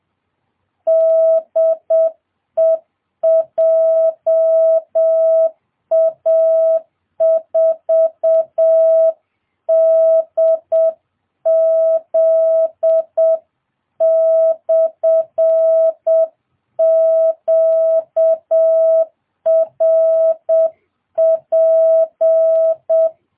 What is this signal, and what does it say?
Morse code, heard as a keyed tone, about 7 wpm, 660 Hz. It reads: DE JA4DZ/QRP